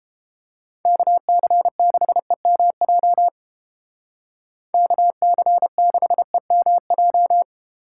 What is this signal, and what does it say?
Morse code, heard as a keyed tone, about 30 wpm, 700 Hz. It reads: KC6EMJ KC6EMJ